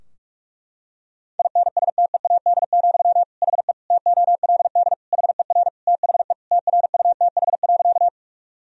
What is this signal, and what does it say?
Morse code, heard as a keyed tone, about 46 wpm, 700 Hz. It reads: INSTEAD, HE TOLD HER THE TRUTH.
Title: Sentences from Top 1000 Words - Review Part 4 - 45wpm